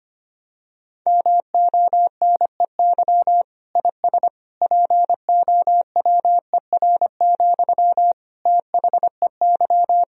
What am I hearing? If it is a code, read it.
Morse code, heard as a keyed tone, about 25 wpm, 700 Hz. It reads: MONEY IS POWER, THEY
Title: Sentences from Top 1000 Words - Review Part 2 - 25wpm